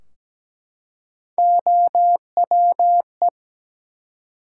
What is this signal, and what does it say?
Morse code, heard as a keyed tone, about 17 wpm, 700 Hz. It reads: OWE
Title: Three Letter Words - 17wpm